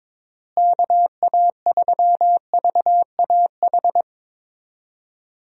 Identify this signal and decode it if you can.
Morse code, heard as a keyed tone, about 22 wpm, 700 Hz. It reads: KA3VAH
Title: USA Call Signs - 6 Characters - Repeated 2x - 22wpm